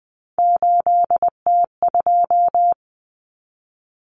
Morse code, heard as a keyed tone, about 20 wpm, 700 Hz. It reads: 8T2
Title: Common Cut Numbers - 3-Digit 20wpm